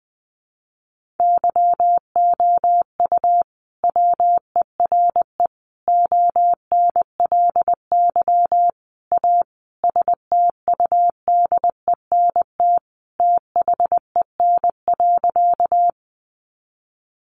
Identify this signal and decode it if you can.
Morse code, heard as a keyed tone, about 20 wpm, 700 Hz. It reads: YOU WERE ONLY A STUDENT THEN.
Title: Sentences from Top 1000 Words 20wpm